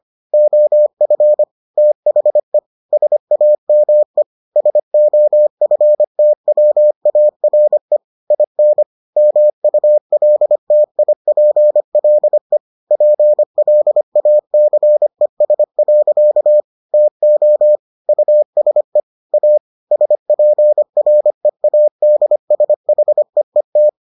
Morse code, heard as a keyed tone, about 25 wpm, 600 Hz. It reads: OF THE SAME SOFTWARE IN MULTIPLE PLACES. TO USE A SPREADSHEET